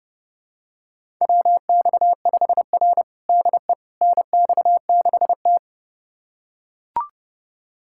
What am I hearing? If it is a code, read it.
Morse code, heard as a keyed tone, about 30 wpm, 700 Hz. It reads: WX5R DE NX6T
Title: USA Callsign DE Callsign - 4 Characters - 30wpm